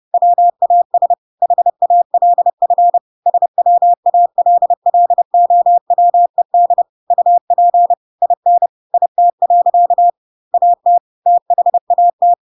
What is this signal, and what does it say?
Morse code, heard as a keyed tone, about 30 wpm, 700 Hz. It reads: WAS HALF SWALLOWED UP IN IT. AT THAT
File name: Pinocchio / Pinocchio ch3 30 30 ik1hge